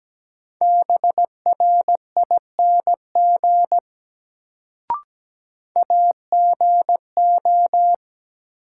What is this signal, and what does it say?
Morse code, heard as a keyed tone, about 17 wpm, 700 Hz. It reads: BRING AGO
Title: Sentences from Top 500 Words - New Word Drill - 17wpm